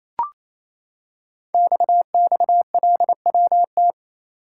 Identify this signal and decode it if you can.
Morse code, heard as a keyed tone, about 29 wpm, 700 Hz. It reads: XXLWT
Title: Code Groups - 5 Letters - 28wpm